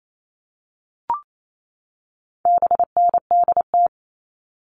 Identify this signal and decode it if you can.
Morse code, heard as a keyed tone, about 28 wpm, 700 Hz. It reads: BNDT
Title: All Cut Numbers - 4-Digit 28wpm